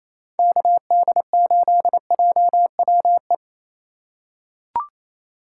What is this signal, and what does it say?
Morse code, heard as a keyed tone, about 28 wpm, 700 Hz. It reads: KD8JWE
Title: USA Call Signs - 6 Characters - 3x Character Spacing - 28wpm